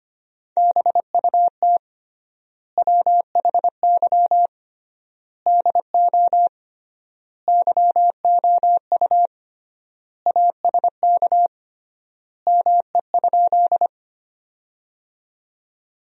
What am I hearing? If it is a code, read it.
Morse code, heard as a keyed tone, about 25 wpm, 700 Hz. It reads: BUT WHY DO YOU ASK ME?
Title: 3x Word-Spacing - Sentences from Top 200 Words - 25wpm